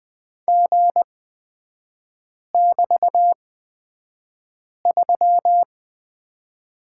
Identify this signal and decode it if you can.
Morse code, heard as a keyed tone, about 20 wpm, spaced as for 5 wpm, 700 Hz. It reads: G<BT>3
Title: 224 Lesson 36e - BT - Three Characters 20wpm